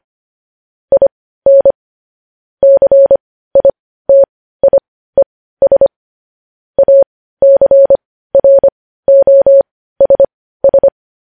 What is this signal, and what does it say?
Morse code, heard as a keyed tone, about 25 wpm, spaced as for 15 wpm, 550 Hz. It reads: IN CITIES ACROSS
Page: news-headlines-in-morse-code-at-15-wpm